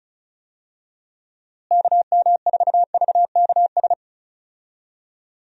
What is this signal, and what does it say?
Morse code, heard as a keyed tone, about 34 wpm, 700 Hz. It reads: KM4VKS
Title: USA Call Signs - 6 Characters - Repeated 2x - 35wpm